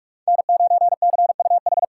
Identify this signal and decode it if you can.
Morse code, heard as a keyed tone, about 39 wpm, 700 Hz. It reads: N9CUH